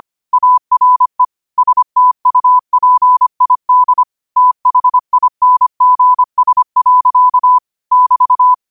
Morse code, heard as a keyed tone, about 25 wpm, 1000 Hz. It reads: ARE STUPID THINGS. <BT>
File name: qotdfiles / 25wpm